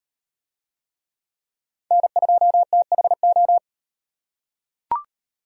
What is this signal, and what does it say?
Morse code, heard as a keyed tone, about 35 wpm, 700 Hz. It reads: N2THO